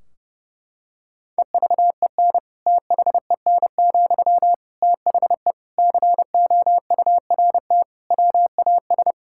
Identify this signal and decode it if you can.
Morse code, heard as a keyed tone, about 30 wpm, 700 Hz. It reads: EVEN THEN, THE COURT WAS
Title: Sentences from Top 1600 Words 30wpm